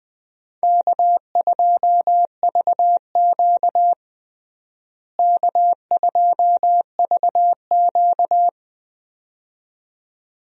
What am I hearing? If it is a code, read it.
Morse code, heard as a keyed tone, about 20 wpm, 700 Hz. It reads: K2VQ K2VQ
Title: USA Call Signs - 4 Characters - Repeated 2x - 20wpm